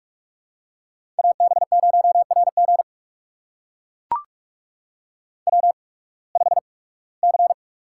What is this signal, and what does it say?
Morse code, heard as a keyed tone, about 44 wpm, spaced as for 23 wpm, 700 Hz. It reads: AB0RG W5C
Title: USA Call Signs - 5 Characters - 8x Character Spacing - 45wpm